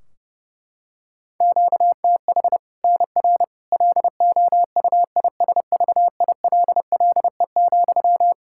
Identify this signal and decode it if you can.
Morse code, heard as a keyed tone, about 32 wpm, 700 Hz. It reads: QTH NR LOUISVILLE,
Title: K1USN SST QSO Practice 30wpm